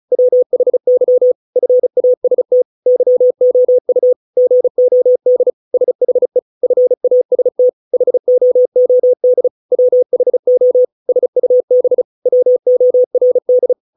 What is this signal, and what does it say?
Morse code, heard as a keyed tone, about 34 wpm, 500 Hz. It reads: WHY FAST YOU GOD SHE FAST HOOD WHO SUB WORD